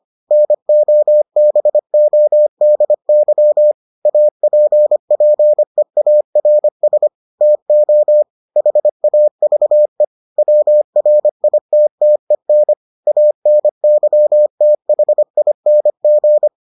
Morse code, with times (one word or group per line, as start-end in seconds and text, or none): 0.3-3.7 NOBODY
4.1-7.1 APPEARS
7.4-8.2 TO
8.6-10.0 HAVE
10.4-12.7 WRITTEN
13.1-16.5 ANYTHING